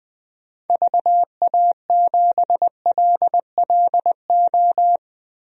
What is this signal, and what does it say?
Morse code, heard as a keyed tone, about 20 wpm, 700 Hz. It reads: VA7LLO
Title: DXCC-100 Call Signs - 6 Characters - Repeated 2x - 20wpm